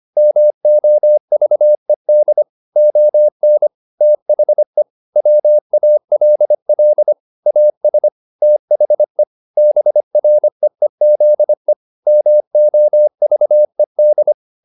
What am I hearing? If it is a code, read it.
Morse code, heard as a keyed tone, about 25 wpm, 600 Hz. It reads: MOVED ON THE WALL AS THE BREEZE MOVED